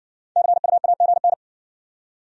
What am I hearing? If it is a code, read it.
Morse code, heard as a keyed tone, about 59 wpm, 700 Hz. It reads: CUAGN